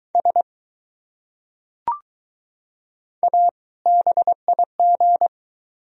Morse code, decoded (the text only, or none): SA BIG